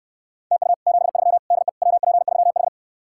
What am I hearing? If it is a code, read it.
Morse code, heard as a keyed tone, about 68 wpm, 700 Hz. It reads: TU 73 DE WP3H